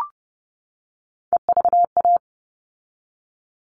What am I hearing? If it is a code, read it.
Morse code, heard as a keyed tone, about 30 wpm, 700 Hz. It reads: EVA
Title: All Cut Numbers - 3-Digit 30wpm